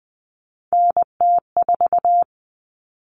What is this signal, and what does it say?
Morse code, heard as a keyed tone, about 20 wpm, 700 Hz. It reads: NT4
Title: Common Cut Numbers - 3-Digit 20wpm